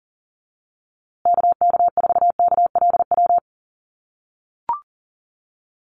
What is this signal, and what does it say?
Morse code, heard as a keyed tone, about 40 wpm, 700 Hz. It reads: KK4KLW
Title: USA Call Signs Repeated 3x 40wpm